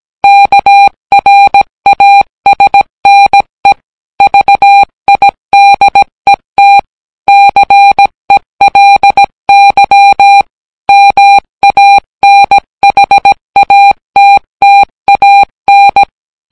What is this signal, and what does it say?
Morse code, heard as a keyed tone, about 17 wpm, 800 Hz. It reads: KRASNE VIDET CELY MANHATTAN